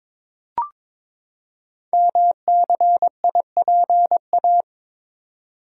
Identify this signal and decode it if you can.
Morse code, heard as a keyed tone, about 22 wpm, 700 Hz. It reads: MCIPA
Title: Code Groups - 5 Letters - 22wpm